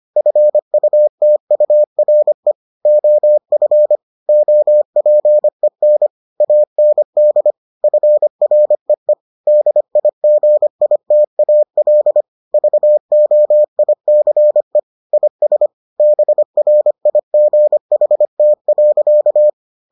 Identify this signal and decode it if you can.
Morse code, heard as a keyed tone, about 25 wpm, 600 Hz. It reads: FUTURE OF OPEN AND FREE DIGITAL VOICE IS BRIGHT.